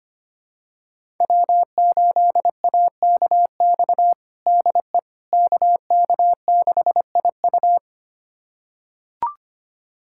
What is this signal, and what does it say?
Morse code, heard as a keyed tone, about 25 wpm, 700 Hz. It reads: W8AKX DE KK6IU
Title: USA Callsign DE Callsign - 5 Characters - 25wpm